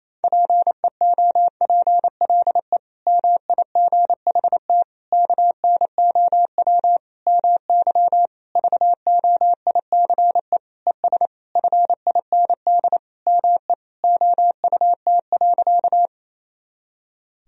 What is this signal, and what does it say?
Morse code, heard as a keyed tone, about 28 wpm, 700 Hz. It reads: PEOPLE MIGHT KNOW MY VOICE ES FIND ME OUT.
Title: Sentences from Top 500 Words and Abbreviations 28wpm